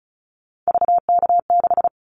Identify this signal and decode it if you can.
Morse code, heard as a keyed tone, about 36 wpm, 700 Hz. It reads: VK6
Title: Callsign Prefixes for Top 50 DXCC - 35wpm